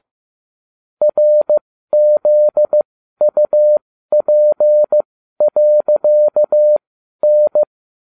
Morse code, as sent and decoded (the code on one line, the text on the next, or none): .-. --.. ..- .--. .-.-.- -.
RZUP.N